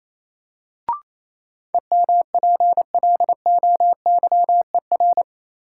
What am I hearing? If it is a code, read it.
Morse code, heard as a keyed tone, about 28 wpm, 700 Hz. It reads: EMPLOYER
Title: Sentences from Top 1600 Words - New Word Drill Alt - 28wpm